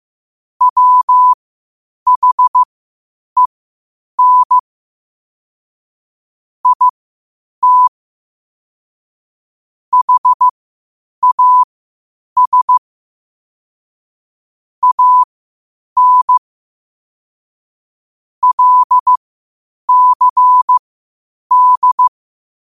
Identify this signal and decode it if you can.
Morse code, heard as a keyed tone, about 15 wpm, spaced as for 8 wpm, 1000 Hz. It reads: WHEN IT HAS AN LCD
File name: qotdfiles / 8wpm